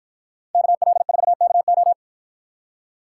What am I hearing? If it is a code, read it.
Morse code, heard as a keyed tone, about 53 wpm, 700 Hz. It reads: KL4KO